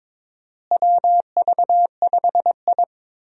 Morse code, heard as a keyed tone, about 22 wpm, 700 Hz. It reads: WV5I
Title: USA Call Signs - 4 Characters - Repeated 2x - 22wpm